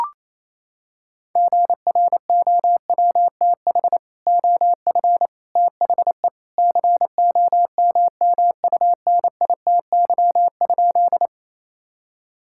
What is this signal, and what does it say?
Morse code, heard as a keyed tone, about 28 wpm, 700 Hz. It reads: GROWTH OF THE COMMUNITY?
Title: Sentences from Top 1000 Words - Review Part 5 - 28wpm